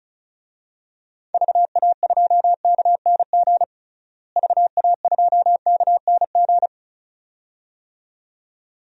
Morse code, heard as a keyed tone, about 36 wpm, 700 Hz. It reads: VA2KNG VA2KNG
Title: DXCC-100 Call Signs - 6 Characters - Repeated 2x - 35wpm